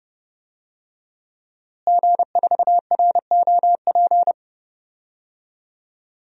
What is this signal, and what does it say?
Morse code, heard as a keyed tone, about 30 wpm, 700 Hz. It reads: G4ROP
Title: DXCC-100 Call Signs - 5 Characters - Repeated 2x - 30wpm